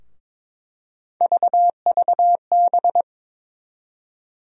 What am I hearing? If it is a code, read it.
Morse code, heard as a keyed tone, about 23 wpm, 700 Hz. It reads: VVB
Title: USA Call Sign Postfixes - 3 Letters 22wpm